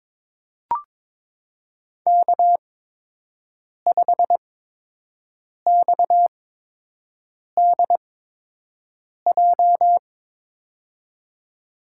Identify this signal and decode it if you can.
Morse code, heard as a keyed tone, about 22 wpm, spaced as for 6 wpm, 700 Hz. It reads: K5XDJ
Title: USA Call Signs - 5 Characters - 8x Character Spacing - 22wpm